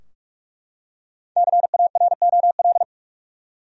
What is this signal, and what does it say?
Morse code, heard as a keyed tone, about 44 wpm, 700 Hz. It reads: CAROL
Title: USA Common Names - Max 5 Letters - 45wpm